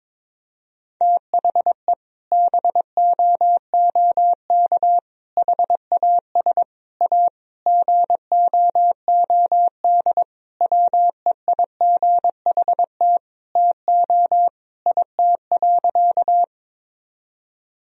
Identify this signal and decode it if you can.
Morse code, heard as a keyed tone, about 21 wpm, 700 Hz. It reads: THE BOOK HAS A GOOD WEIGHT TO IT.
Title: Sentences from Top 1000 Words - Review Part 3 - 22wpm